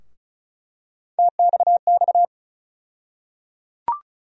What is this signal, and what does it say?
Morse code, heard as a keyed tone, about 34 wpm, 700 Hz. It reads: TXX
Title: USA Call Sign Postfixes - 3 Letters 35wpm